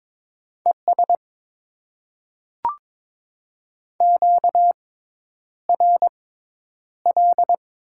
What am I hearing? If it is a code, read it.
Morse code, heard as a keyed tone, about 22 wpm, spaced as for 10 wpm, 700 Hz. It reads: ES QRL
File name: Top 20 QSO Elements - 6x Character Spacing - 22wpm